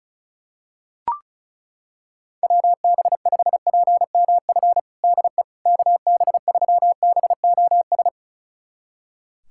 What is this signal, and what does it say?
Morse code, heard as a keyed tone, about 36 wpm, 700 Hz. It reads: WB5PMF DE KB3BOS
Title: USA Callsign DE Callsign 35wpm